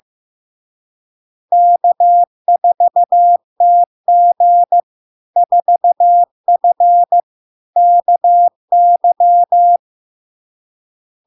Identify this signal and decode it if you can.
Morse code, heard as a keyed tone, about 15 wpm, 700 Hz. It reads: K4TG 4F KY
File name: LARK-FD-2022-Log-in-CW-15wpm-700-Hz